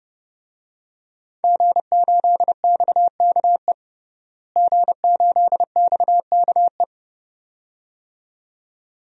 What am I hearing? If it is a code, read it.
Morse code, heard as a keyed tone, about 30 wpm, 700 Hz. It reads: G8XKE G8XKE
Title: DXCC-100 Call Signs - 5 Characters - Repeated 2x - 30wpm